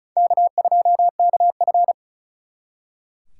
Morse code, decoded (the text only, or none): K2KF